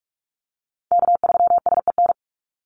Morse code, heard as a keyed tone, about 44 wpm, 700 Hz. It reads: K3SEN
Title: Call Signs Repeated 2x - 45wpm